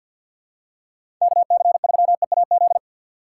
Morse code, heard as a keyed tone, about 53 wpm, 700 Hz. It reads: KC3EAZ
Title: USA Call Signs - 6 Characters - 3x Character Spacing - 50wpm